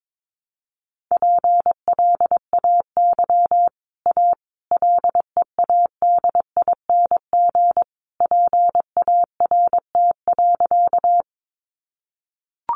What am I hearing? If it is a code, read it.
Morse code, heard as a keyed tone, about 22 wpm, 700 Hz. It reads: PLAY A LEADING PART.
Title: Sentences from Top 1100 Words 22wpm